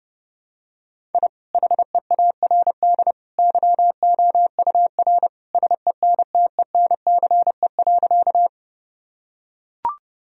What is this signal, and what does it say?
Morse code, heard as a keyed tone, about 30 wpm, 700 Hz. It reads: I HEARD YOUR SENTENCE.
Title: Sentences from Top 1000 Words - Review Part 1 - 30wpm